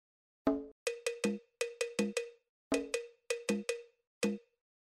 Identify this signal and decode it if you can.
There is no code.